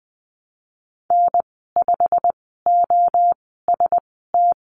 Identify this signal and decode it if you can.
Morse code, heard as a keyed tone, about 20 wpm, spaced as for 15 wpm, 700 Hz. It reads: N5OST
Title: 088 Lesson 15g - Number 5 - Call Signs 20wpm